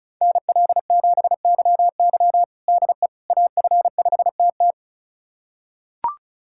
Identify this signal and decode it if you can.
Morse code, heard as a keyed tone, about 34 wpm, 700 Hz. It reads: NL7YY DE AF5TT
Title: USA Callsign DE Callsign - 5 Characters - 35wpm